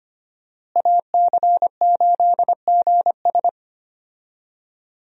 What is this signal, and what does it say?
Morse code, heard as a keyed tone, about 25 wpm, 700 Hz. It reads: AC8GS
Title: USA Call Signs - 5 Characters - Repeated 2x - 25wpm